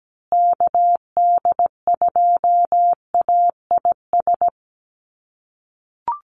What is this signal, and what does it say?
Morse code, heard as a keyed tone, about 17 wpm, 700 Hz. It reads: KD2AIS
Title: USA Call Signs Repeated 2x 17wpm